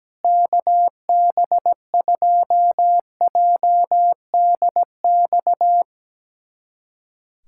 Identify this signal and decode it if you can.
Morse code, heard as a keyed tone, about 17 wpm, 700 Hz. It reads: KB2JDX
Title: USA Callsign DE Callsign 17wpm